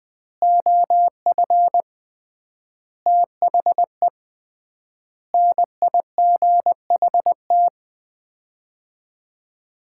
Morse code, heard as a keyed tone, about 20 wpm, 700 Hz. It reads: OF THE NIGHT
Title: 3x Word-Spacing - Sets of 5 Words - 20wpm